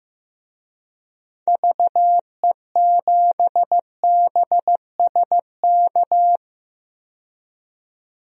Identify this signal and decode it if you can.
Morse code, heard as a keyed tone, about 15 wpm, 700 Hz. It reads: VE7BSK
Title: DXCC-100 Call Signs - 6 Characters - 15wpm